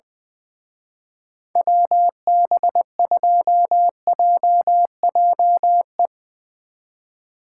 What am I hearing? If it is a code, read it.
Morse code, heard as a keyed tone, about 21 wpm, 700 Hz. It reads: WB2JJE
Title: Speed Racing - USA 2x3 Call Signs 20wpm